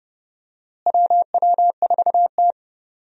Morse code, heard as a keyed tone, about 30 wpm, 700 Hz. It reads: WW4T
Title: USA Call Signs - 4 Characters - Repeated 3x - 30wpm